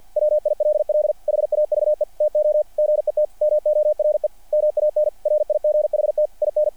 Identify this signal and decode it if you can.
Morse code, heard as a keyed tone, about 50 wpm, 600 Hz. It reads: WILL HAVE TO GET MORE MAN RIGHT IN
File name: archived / CRSnet